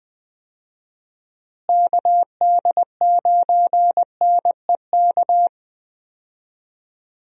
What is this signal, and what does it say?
Morse code, heard as a keyed tone, about 20 wpm, 700 Hz. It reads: KD9NEK
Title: USA Call Signs - 6 Characters - Repeated 3x - 20wpm